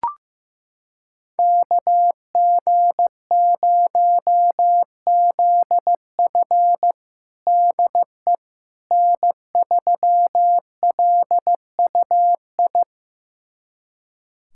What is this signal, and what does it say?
Morse code, heard as a keyed tone, about 15 wpm, 700 Hz. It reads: KG0ZF DE N3LUI
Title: USA Callsign DE Callsign - 5 Characters - 15wpm